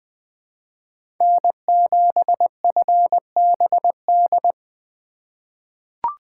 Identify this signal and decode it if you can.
Morse code, heard as a keyed tone, about 20 wpm, 700 Hz. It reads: N7FBD
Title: USA Call Signs - 5 Characters - 2x Character Spacing - 20wpm